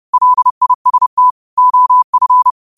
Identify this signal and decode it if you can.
Morse code, heard as a keyed tone, about 30 wpm, 1000 Hz. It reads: LIST OF
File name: qotdfiles / 30wpm